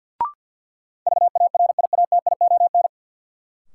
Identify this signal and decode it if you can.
Morse code, heard as a keyed tone, about 46 wpm, 700 Hz. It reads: VARIATION
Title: Sentences from Top 1600 Words - New Word Drill Alt - 50wpm